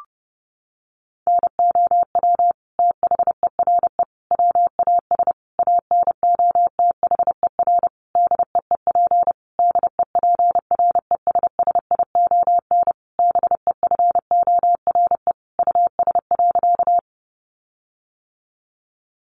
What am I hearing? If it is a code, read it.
Morse code, heard as a keyed tone, about 30 wpm, 700 Hz. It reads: NOW THERE WAS ANOTHER DEEP DEPRESSION BEFORE US.
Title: Sentences from Top 1300 Words 30wpm